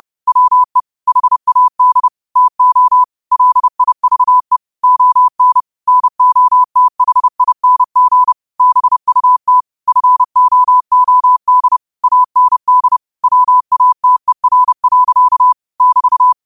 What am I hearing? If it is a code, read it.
Morse code, heard as a keyed tone, about 30 wpm, 1000 Hz. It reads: WE HAD TO LIVE ON NOTHING BUT FOOD AND WATER. <BT>
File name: qotdfiles / 30wpm